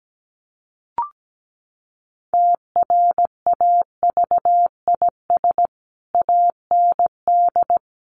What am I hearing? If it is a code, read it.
Morse code, heard as a keyed tone, about 17 wpm, 700 Hz. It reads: TRAVIS AND
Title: A Sound Of Thunder 17wpm